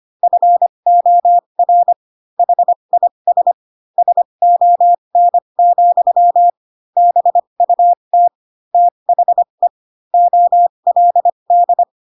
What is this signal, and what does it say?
Morse code, heard as a keyed tone, about 25 wpm, spaced as for 22 wpm, 700 Hz. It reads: FOR HIS SON, BUT THE OLD